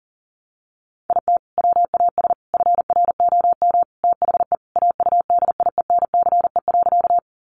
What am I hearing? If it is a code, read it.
Morse code, heard as a keyed tone, about 40 wpm, 700 Hz. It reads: IT WAS FROM THE AUDIENCE.
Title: Sentences from Top 800 Words 40wpm